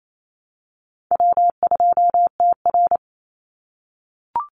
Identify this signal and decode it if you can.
Morse code, heard as a keyed tone, about 28 wpm, 700 Hz. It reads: W2TR
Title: Call Signs - 28wpm